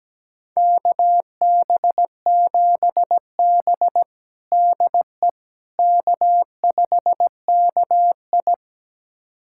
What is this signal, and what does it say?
Morse code, heard as a keyed tone, about 17 wpm, 700 Hz. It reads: KB7B DE K5KI
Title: USA Callsign DE Callsign - 4 Characters - 17wpm